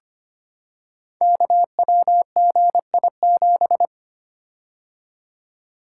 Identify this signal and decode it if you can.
Morse code, heard as a keyed tone, about 25 wpm, 700 Hz. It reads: KWGI7
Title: Code Groups - 5 Letter:Numbers - 25wpm